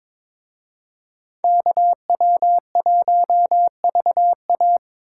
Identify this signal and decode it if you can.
Morse code, heard as a keyed tone, about 26 wpm, 700 Hz. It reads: KW1VA
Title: Speed Racing - USA 2x2 Call Signs 22wpm